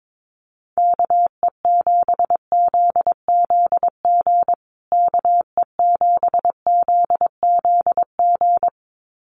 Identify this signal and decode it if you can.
Morse code, heard as a keyed tone, about 22 wpm, 700 Hz. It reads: KE7ZZG KE7ZZG
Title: USA Call Signs Repeated 2x 22wpm